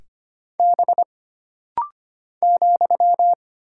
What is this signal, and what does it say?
Morse code, heard as a keyed tone, about 25 wpm, 700 Hz. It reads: B ,
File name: Single Character (Letter:Number:Punctuation:Prosign) - Rapid-Fire - 25wpm